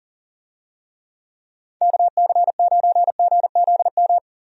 Morse code, heard as a keyed tone, about 40 wpm, 700 Hz. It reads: KC9GZM
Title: USA Call Signs - 6 Characters - Repeated 3x - 40wpm